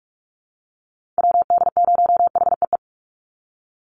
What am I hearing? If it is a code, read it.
Morse code, heard as a keyed tone, about 44 wpm, 700 Hz. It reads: WD0HEE
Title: USA Call Signs 45wpm